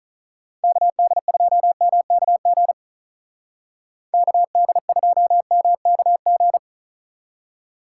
Morse code, heard as a keyed tone, about 44 wpm, 700 Hz. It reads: KD2MKG KD2MKG
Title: Speed Racing - USA 2x3 Call Signs 35wpm